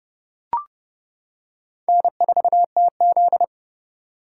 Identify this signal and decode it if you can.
Morse code, heard as a keyed tone, about 30 wpm, 700 Hz. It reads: N4TZ
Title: K1USN SST QSO Practice - 30wpm